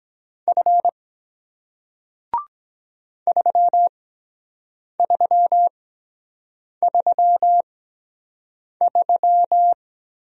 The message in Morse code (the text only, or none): F 3333